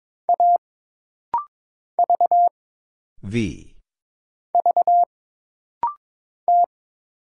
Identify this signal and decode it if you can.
Morse code, heard as a keyed tone, about 22 wpm, 700 Hz. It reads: A V V T
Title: Single Letter-Number Alt - Rapid-Fire 22wpm